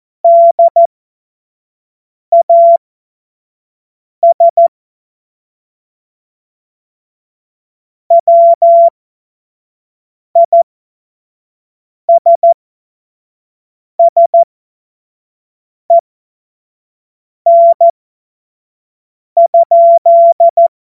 Morse code, heard as a keyed tone, about 14 wpm, spaced as for 5 wpm, 680 Hz. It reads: DAS WISSEN?
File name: Morsekurs / Witze / Witz 22. Februar 2022 25 F70